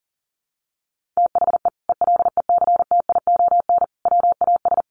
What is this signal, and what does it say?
Morse code, heard as a keyed tone, about 40 wpm, 700 Hz. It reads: THE ELECTION WAS
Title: Sentences from Top 1500 Words 40wpm